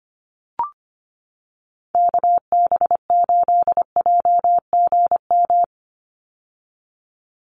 Morse code, heard as a keyed tone, about 25 wpm, 690 Hz. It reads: KB8JGM
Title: USA Call Signs 25wpm